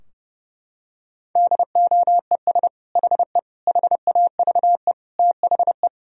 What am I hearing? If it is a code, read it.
Morse code, heard as a keyed tone, about 30 wpm, 700 Hz. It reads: DOES HE HAVE THE
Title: Sentences from Top 1000 Words - Review Part 3 - 30wpm